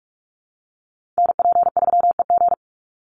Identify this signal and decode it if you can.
Morse code, heard as a keyed tone, about 44 wpm, 700 Hz. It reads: NP3EG
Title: USA Call Signs 45wpm